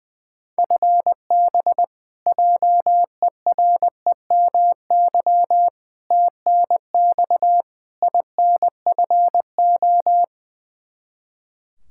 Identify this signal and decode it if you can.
Morse code, heard as a keyed tone, about 20 wpm, 700 Hz. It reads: FB JEREMY TNX INFO
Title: Practice QSO - Exchange 3 - More Info - 20wpm